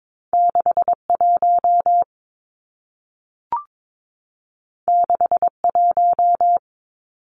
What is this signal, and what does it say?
Morse code, heard as a keyed tone, about 22 wpm, 700 Hz. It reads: 61 61
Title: Numbers - 2-Digit 22wpm